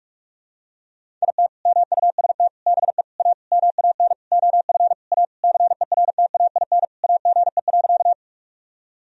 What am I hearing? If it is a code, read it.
Morse code, heard as a keyed tone, about 46 wpm, 700 Hz. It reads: IT MUST BE A MAN OF A CERTAIN AGE.
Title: Sentences from Top 1000 Words - Review Part 3 - 45wpm